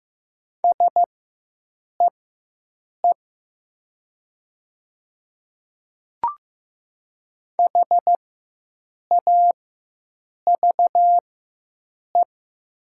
Morse code, heard as a keyed tone, about 15 wpm, spaced as for 7 wpm, 700 Hz. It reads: SEE HAVE
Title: 4x Character Spacing - Top 100 Words - 15wpm